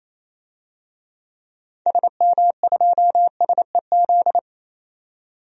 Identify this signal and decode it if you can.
Morse code, heard as a keyed tone, about 28 wpm, 700 Hz. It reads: SM2SEZ
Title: DXCC-100 Call Signs - 6 Characters - 28wpm